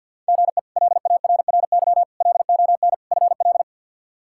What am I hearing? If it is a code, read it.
Morse code, heard as a keyed tone, about 47 wpm, 700 Hz. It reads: GE LARRY LON FL
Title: K1USN SST QSO Practice - 50wpm